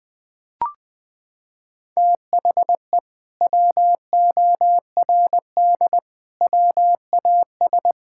Morse code, heard as a keyed tone, about 21 wpm, 700 Hz. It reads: THE WORD WAS
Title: Sentences from Top 1000 Words - Review Part 5 - 20wpm